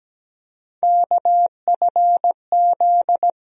Morse code, heard as a keyed tone, about 17 wpm, 700 Hz. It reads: KFZ
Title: USA Call Sign Postfixes - 3 Letters 17wpm